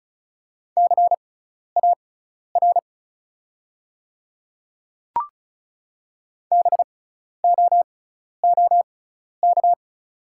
Morse code, heard as a keyed tone, about 36 wpm, spaced as for 12 wpm, 700 Hz. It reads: CAR BOOK
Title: 6x Character Spacing - Top 500 Words - 35wpm